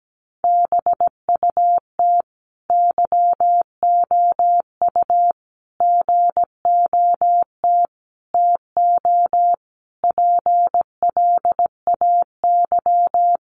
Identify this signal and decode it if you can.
Morse code, heard as a keyed tone, about 17 wpm, 700 Hz. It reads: BUT YOU GOT TO PLAY
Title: Sentences from Top 300 Words 17wpm